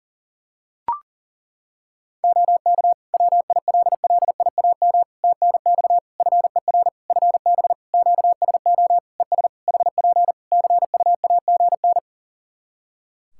Morse code, heard as a keyed tone, about 40 wpm, 700 Hz. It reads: OK WILLIAM TNX FER FB QSO ES HP CUAGN
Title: Practice QSO - Exchange 4 - Ending - 40wpm